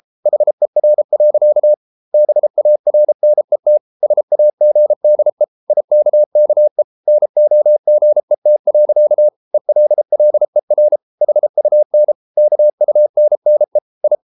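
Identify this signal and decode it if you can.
Morse code, heard as a keyed tone, about 33 wpm, 600 Hz. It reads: HER. BARNET SAGDE IKKE NOGET. ELLER HUN KUNNE I
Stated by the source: Morse code at 33 wpm, QRQ.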